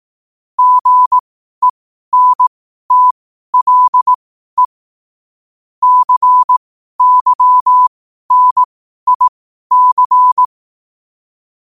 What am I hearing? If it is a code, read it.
Morse code, heard as a keyed tone, about 18 wpm, spaced as for 12 wpm, 1000 Hz. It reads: GENTLE CYNIC
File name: qotdfiles / 12wpm